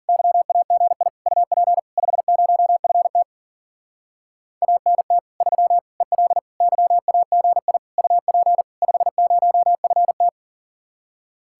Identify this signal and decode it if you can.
Morse code, heard as a keyed tone, about 47 wpm, 700 Hz. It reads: YAGI UP 50FT ANT 3 EL YAGI UP 50FT
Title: Speed-Racing Common QSO Phrases - 40wpm